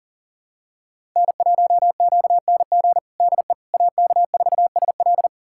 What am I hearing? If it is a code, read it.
Morse code, heard as a keyed tone, about 40 wpm, 700 Hz. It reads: N1QNG DE AK4SL
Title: USA Callsign DE Callsign - 5 Characters - 40wpm